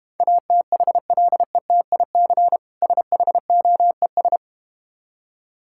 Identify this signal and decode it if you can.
Morse code, heard as a keyed tone, about 32 wpm, 700 Hz. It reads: ATHLETIC SHOES